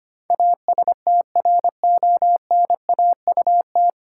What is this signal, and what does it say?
Morse code, heard as a keyed tone, about 25 wpm, 700 Hz. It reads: ASTRONAUT
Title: Pangram Challenge v2 - 25wpm